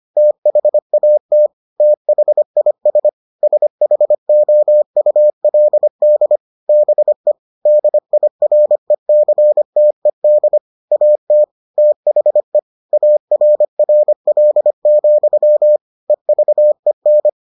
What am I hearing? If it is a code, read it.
Morse code, heard as a keyed tone, about 25 wpm, 600 Hz. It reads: THAT THIS SHOULD BE DIRECTED AT THE ARRL, EVEN